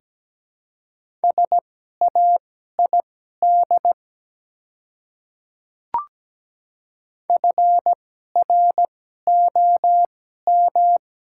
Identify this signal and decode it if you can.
Morse code, heard as a keyed tone, about 17 wpm, spaced as for 12 wpm, 700 Hz. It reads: SAID FROM
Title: 2x Character Spacing - Top 100 Words - 17wpm